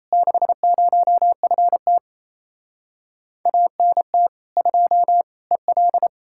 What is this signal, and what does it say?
Morse code, heard as a keyed tone, about 37 wpm, 700 Hz. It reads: 60FT ANT 2 EL